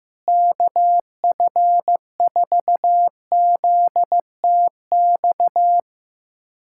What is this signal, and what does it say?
Morse code, heard as a keyed tone, about 15 wpm, 700 Hz. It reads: KF4ZTX
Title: USA Callsign DE Callsign 15wpm